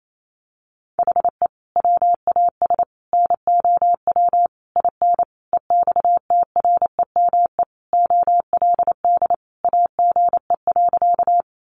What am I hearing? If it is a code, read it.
Morse code, heard as a keyed tone, about 28 wpm, 700 Hz. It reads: HE WAS NOW IN EXTREME OLD AGE.
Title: Sentences from Top 1300 Words 28wpm